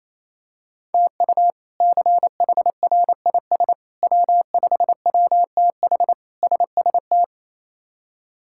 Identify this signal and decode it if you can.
Morse code, heard as a keyed tone, about 28 wpm, 700 Hz. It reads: TU CHRIS W5WTH SST
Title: K1USN SST QSO Practice - 28wpm